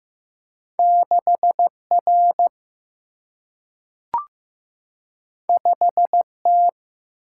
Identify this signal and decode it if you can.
Morse code, heard as a keyed tone, about 15 wpm, 700 Hz. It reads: 6R 5T
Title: Single-Letter Single-Number 15wpm